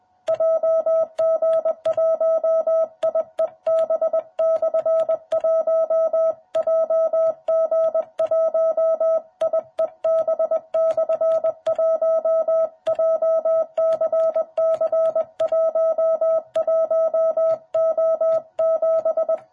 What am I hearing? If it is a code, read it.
Morse code, heard as a keyed tone, about 20 wpm, 660 Hz. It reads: JG1IEB/1 JG1IEB/1 JCC11O7
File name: jh8cbh / 2021cw